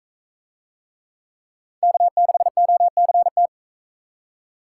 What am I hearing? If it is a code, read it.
Morse code, heard as a keyed tone, about 44 wpm, 700 Hz. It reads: K6OCT